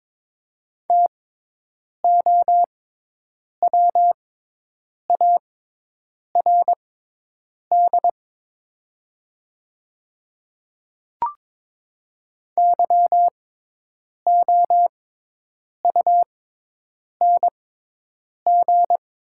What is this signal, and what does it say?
Morse code, heard as a keyed tone, about 22 wpm, spaced as for 8 wpm, 690 Hz. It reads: TOWARD YOUNG
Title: 6x Character Spacing - Top 500 Words - 22wpm